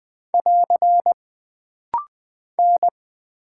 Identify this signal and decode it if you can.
Morse code, heard as a keyed tone, about 20 wpm, 700 Hz. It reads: <AR> N
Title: Single Character (Letter:Number:Punctuation:Prosign) - Rapid-Fire - 20wpm